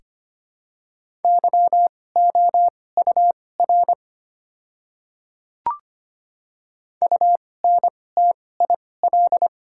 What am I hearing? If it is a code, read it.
Morse code, heard as a keyed tone, about 25 wpm, spaced as for 18 wpm, 690 Hz. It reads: YOUR UNTIL
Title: 2x Character Spacing - Top 300 Words - 25wpm